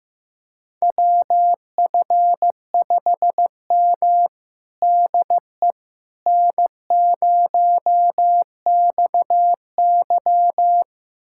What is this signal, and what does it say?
Morse code, heard as a keyed tone, about 15 wpm, 700 Hz. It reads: WF5M DE N0XY
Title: USA Callsign DE Callsign - 4 Characters - 15wpm